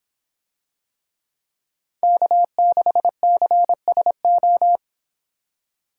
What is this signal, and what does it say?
Morse code, heard as a keyed tone, about 24 wpm, 700 Hz. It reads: K6CSO